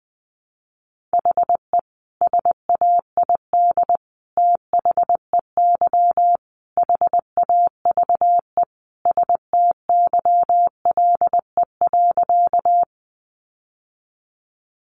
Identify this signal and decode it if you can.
Morse code, heard as a keyed tone, about 20 wpm, 700 Hz. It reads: HE SAID THEY HAVE STYLE.
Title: Sentences from Top 800 Words 20wpm